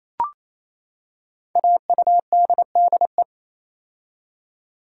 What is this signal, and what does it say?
Morse code, heard as a keyed tone, about 27 wpm, 700 Hz. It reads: AUDDE
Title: Code Groups - 5 Letters - 28wpm